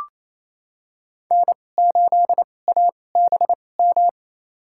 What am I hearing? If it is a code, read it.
Morse code, heard as a keyed tone, about 28 wpm, spaced as for 20 wpm, 700 Hz. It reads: N8ABM
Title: USA Call Signs - 2x Character Spacing - 28wpm